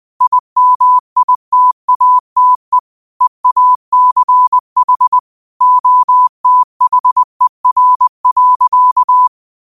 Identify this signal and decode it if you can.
Morse code, heard as a keyed tone, about 20 wpm, 1000 Hz. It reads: IMITATE EACH OTHER.